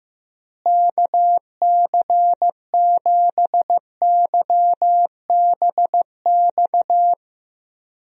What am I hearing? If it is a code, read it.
Morse code, heard as a keyed tone, about 15 wpm, 700 Hz. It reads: KC7YBX
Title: USA Call Signs - 6 Characters - Repeated 3x - 15wpm